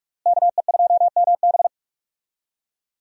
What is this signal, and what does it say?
Morse code, heard as a keyed tone, about 46 wpm, 700 Hz. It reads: KE2MB
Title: USA Call Signs - 5 Characters - Repeated 2x - 45wpm